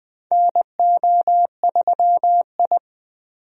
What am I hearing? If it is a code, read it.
Morse code, heard as a keyed tone, about 21 wpm, 700 Hz. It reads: NO3I
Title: USA Call Signs - 4 Characters - 20wpm